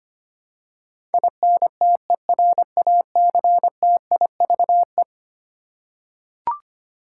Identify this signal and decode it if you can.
Morse code, heard as a keyed tone, about 25 wpm, 700 Hz. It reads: INTERACTIVE
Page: morse-code-ninja-practice-25wpm